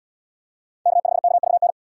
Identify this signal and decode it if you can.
Morse code, heard as a keyed tone, about 71 wpm, 700 Hz. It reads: K5KVN